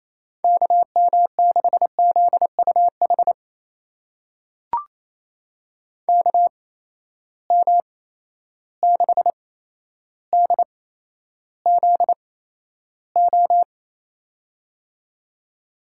Morse code, heard as a keyed tone, about 28 wpm, spaced as for 12 wpm, 700 Hz. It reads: KM6ZUH KM6DZO
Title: USA Call Signs - 6 Characters - 8x Character Spacing - 28wpm